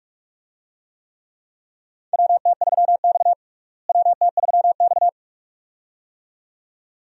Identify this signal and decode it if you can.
Morse code, heard as a keyed tone, about 47 wpm, 700 Hz. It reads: WT3X WT3X
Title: USA Call Signs - 4 Characters - Repeated 2x - 45wpm